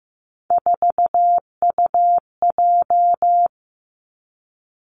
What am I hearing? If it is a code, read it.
Morse code, heard as a keyed tone, about 15 wpm, 700 Hz. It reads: 4UJ
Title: Three Challenging Letter-Number Combinations 15wpm